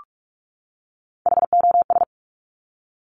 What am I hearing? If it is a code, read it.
Morse code, heard as a keyed tone, about 46 wpm, 690 Hz. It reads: HOS
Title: Top Three Letter Combinations in Words 45wpm